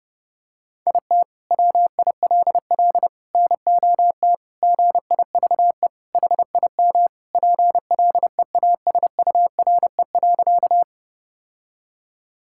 Morse code, heard as a keyed tone, about 30 wpm, 700 Hz. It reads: IT WILL NOT GIVE HIM PLEASURE.
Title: Sentences from Top 1600 Words 30wpm